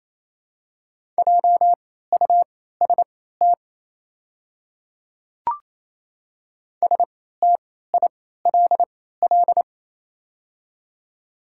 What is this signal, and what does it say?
Morse code, heard as a keyed tone, about 28 wpm, spaced as for 16 wpm, 700 Hz. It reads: JUST STILL
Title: 3x Character Spacing - Top 300 Words - 28wpm